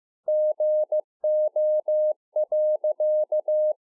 Morse code, heard as a keyed tone, about 15 wpm, 610 Hz. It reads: GO.